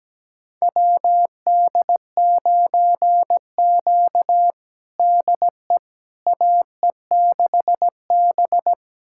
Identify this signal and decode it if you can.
Morse code, heard as a keyed tone, about 17 wpm, 700 Hz. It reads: WD9Q DE AE6B
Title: USA Callsign DE Callsign - 4 Characters - 17wpm